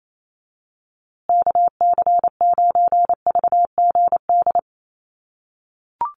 Morse code, heard as a keyed tone, about 28 wpm, 700 Hz. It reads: KC9VGD
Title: USA Call Signs Repeated 3x 28wpm